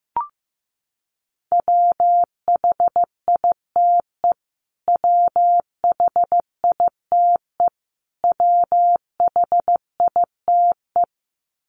Morse code, heard as a keyed tone, about 15 wpm, 700 Hz. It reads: WHITE WHITE WHITE
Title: Top 500 Words Repeated 15wpm